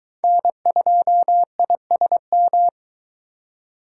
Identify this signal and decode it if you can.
Morse code, heard as a keyed tone, about 23 wpm, 700 Hz. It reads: N2ISM